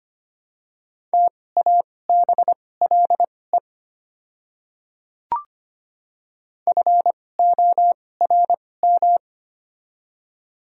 Morse code, heard as a keyed tone, about 25 wpm, spaced as for 18 wpm, 700 Hz. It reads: TABLE FORM
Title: 2x Character Spacing - Top 500 Words - 25wpm